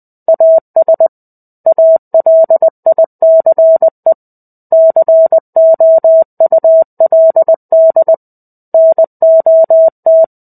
Morse code, heard as a keyed tone, about 20 wpm, 650 Hz. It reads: AS ALICE COULD NOT
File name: 05-Chapter-05 20 WPM